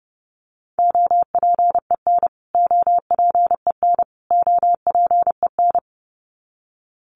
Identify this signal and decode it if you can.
Morse code, heard as a keyed tone, about 29 wpm, 700 Hz. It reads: OPEN OPEN OPEN
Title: Top 300 Words Repeated 30wpm